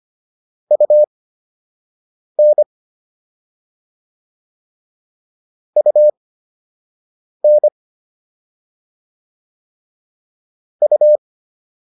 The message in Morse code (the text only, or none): UN UN U